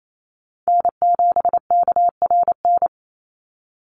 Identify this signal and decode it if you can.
Morse code, heard as a keyed tone, about 28 wpm, 700 Hz. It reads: N7KRN
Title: NA SOTA Chaser Callsigns 28wpm